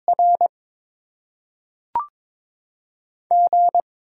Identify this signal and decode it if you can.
Morse code, heard as a keyed tone, about 22 wpm, 700 Hz. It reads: R G
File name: Top 20 QSO Elements - 8x Character Spacing - 22wpm